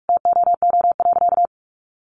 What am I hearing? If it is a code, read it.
Morse code, heard as a keyed tone, about 44 wpm, 700 Hz. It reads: TOO.
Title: Sentences from Top 900 Words 45wpm